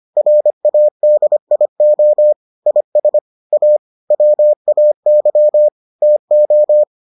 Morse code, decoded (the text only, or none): RADIO IS A WAY TO